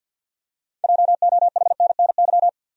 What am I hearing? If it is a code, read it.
Morse code, heard as a keyed tone, about 47 wpm, 700 Hz. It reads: JOHNNY
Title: USA Common Names - 50wpm